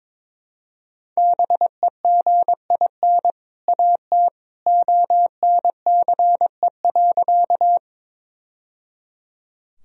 Morse code, heard as a keyed tone, about 22 wpm, 700 Hz. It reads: BEGIN AT ONCE.
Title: Sentences from Top 1000 Words - Review Part 1 - 22wpm